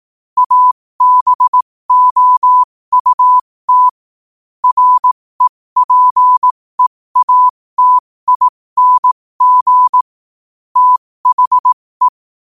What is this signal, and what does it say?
Morse code, heard as a keyed tone, about 18 wpm, spaced as for 15 wpm, 1000 Hz. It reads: ABOUT REPEATING THE
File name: qotdfiles / 15wpm